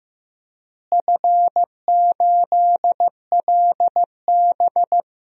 Morse code, heard as a keyed tone, about 15 wpm, 700 Hz. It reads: F8LB
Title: DXCC-100 Call Signs - 4 Characters - Repeated 2x - 15wpm